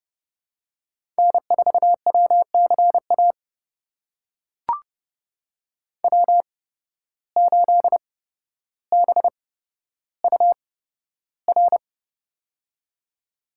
Morse code, heard as a keyed tone, about 30 wpm, spaced as for 13 wpm, 700 Hz. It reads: N4WCA W8BUR
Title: USA Call Signs - 5 Characters - 8x Character Spacing - 30wpm